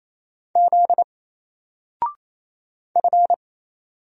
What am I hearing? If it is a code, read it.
Morse code, heard as a keyed tone, about 28 wpm, 700 Hz. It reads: Z F
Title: Single Character (Letter:Number:Punctuation:Prosign) Alt 28wpm